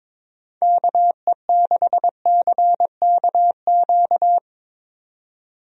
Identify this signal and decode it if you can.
Morse code, heard as a keyed tone, about 22 wpm, 700 Hz. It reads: KE6CKQ
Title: USA Call Signs - 6 Characters - 22wpm